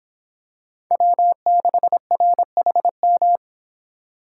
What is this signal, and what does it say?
Morse code, heard as a keyed tone, about 26 wpm, 700 Hz. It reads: W6RHM
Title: Speed Racing - USA 1x3 Call Signs 22wpm